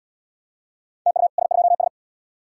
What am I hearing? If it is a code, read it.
Morse code, heard as a keyed tone, about 80 wpm, 700 Hz. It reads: TU S2S